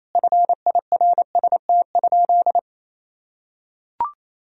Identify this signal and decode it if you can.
Morse code, heard as a keyed tone, about 28 wpm, 700 Hz. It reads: FIRST?
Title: Sentences from Top 1000 Words - Review Part 3 - 28wpm